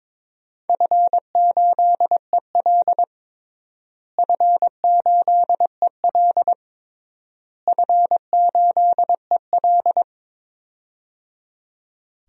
Morse code, heard as a keyed tone, about 22 wpm, 700 Hz. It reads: F8EL F8EL F8EL
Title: DXCC-100 Call Signs - 4 Characters - Repeated 3x - 22wpm